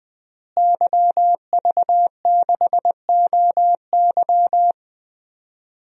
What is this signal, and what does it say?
Morse code, heard as a keyed tone, about 20 wpm, 700 Hz. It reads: YV6OY
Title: DXCC-100 Call Signs - 5 Characters - Repeated 2x - 20wpm